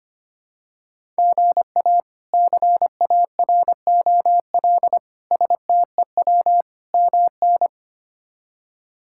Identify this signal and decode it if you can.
Morse code, heard as a keyed tone, about 25 wpm, 700 Hz. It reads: GA CAROL STEW MN
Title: K1USN SST QSO Practice - 25wpm